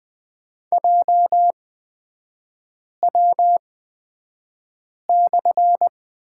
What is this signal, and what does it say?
Morse code, heard as a keyed tone, about 20 wpm, spaced as for 5 wpm, 690 Hz. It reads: JW/
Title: 229 Lesson 37e - Letter x - Three Characters 20wpm